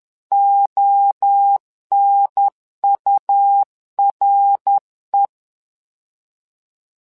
Morse code, heard as a keyed tone, about 10 wpm, 800 Hz. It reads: ONURE
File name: LEZIONI / 3 MWRO / 3 MWRO 10wpm 2148